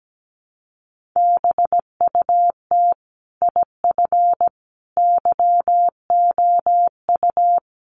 Morse code, heard as a keyed tone, about 17 wpm, 700 Hz. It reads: BUT IF YOU
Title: Sentences from Top 1200 Words 17wpm